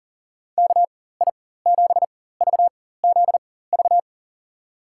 Morse code, heard as a keyed tone, about 40 wpm, spaced as for 19 wpm, 700 Hz. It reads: KI7VZV
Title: USA Call Signs - 6 Characters - 4x Character Spacing - 40wpm